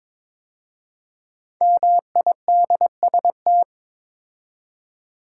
Morse code, heard as a keyed tone, about 22 wpm, 700 Hz. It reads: MIDST
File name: 5 Letter Words - First 3 Letters Given - 22wpm